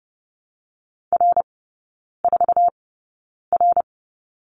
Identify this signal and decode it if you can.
Morse code, heard as a keyed tone, about 30 wpm, spaced as for 9 wpm, 700 Hz. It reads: R4R
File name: Lesson 18e - Letter m - Three Characters 30wpm